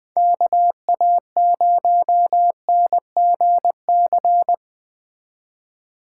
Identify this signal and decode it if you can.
Morse code, heard as a keyed tone, about 20 wpm, 700 Hz. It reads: KA0NGC